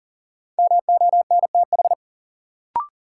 Morse code, heard as a keyed tone, about 40 wpm, 700 Hz. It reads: MONTH
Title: Sets of 5 Words Encore - New Word Drill - 40wpm